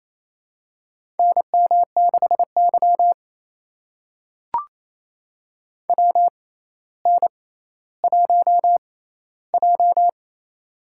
Morse code, heard as a keyed tone, about 27 wpm, spaced as for 14 wpm, 700 Hz. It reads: NM6Y WN1J
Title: USA Call Signs - 4 Characters - 6x Character Spacing - 28wpm